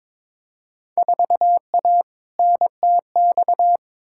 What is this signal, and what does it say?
Morse code, heard as a keyed tone, about 22 wpm, 700 Hz. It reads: 4A NTX
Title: Field Day Exchanges - 22wpm